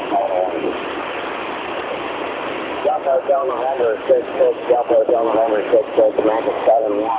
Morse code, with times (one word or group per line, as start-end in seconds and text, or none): none